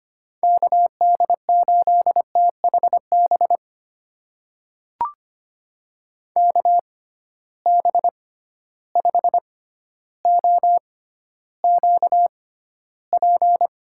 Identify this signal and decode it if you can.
Morse code, heard as a keyed tone, about 25 wpm, spaced as for 13 wpm, 700 Hz. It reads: KD8THB KB5OQP
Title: USA Call Signs - 6 Characters - 6x Character Spacing - 25wpm